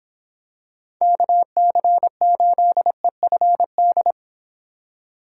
Morse code, heard as a keyed tone, about 26 wpm, 700 Hz. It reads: KC8EFD